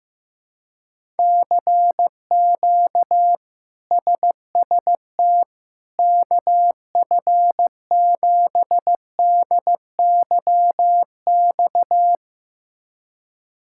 Morse code, heard as a keyed tone, about 16 wpm, 700 Hz. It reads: CQ SST KF7DYX
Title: K1USN SST QSO Practice - 15wpm